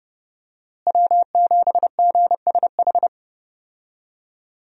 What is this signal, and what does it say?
Morse code, heard as a keyed tone, about 30 wpm, 700 Hz. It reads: W7GSH
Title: USA Call Signs - 5 Characters - Repeated 2x - 30wpm